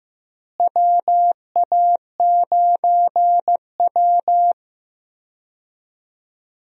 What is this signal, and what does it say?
Morse code, heard as a keyed tone, about 15 wpm, 700 Hz. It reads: WA9W
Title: USA Call Signs - 4 Characters - Repeated 3x - 15wpm